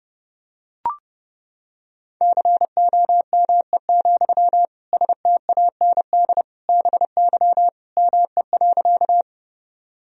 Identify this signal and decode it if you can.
Morse code, heard as a keyed tone, about 32 wpm, 700 Hz. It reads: COME, STAND BY ME.
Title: Sentences from Top 1000 Words - Review Part 1 - 30wpm